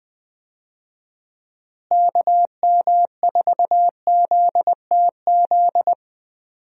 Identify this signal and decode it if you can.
Morse code, heard as a keyed tone, about 20 wpm, 700 Hz. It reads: KM4ZTZ
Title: USA Call Signs - 6 Characters - Repeated 3x - 20wpm